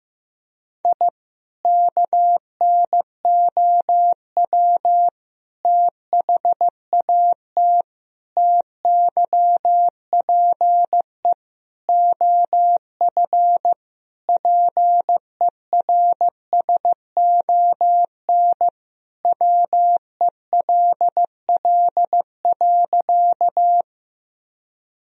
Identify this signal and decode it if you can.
Morse code, heard as a keyed tone, about 15 wpm, 700 Hz. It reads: I KNOW THAT TYPE OF PERSON WELL.
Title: Sentences from Top 1000 Words - Review Part 3 - 15wpm